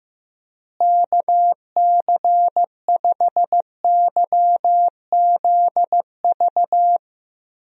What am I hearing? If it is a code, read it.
Morse code, heard as a keyed tone, about 15 wpm, 700 Hz. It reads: KC5YZV
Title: USA Call Signs - 6 Characters - Repeated 3x - 15wpm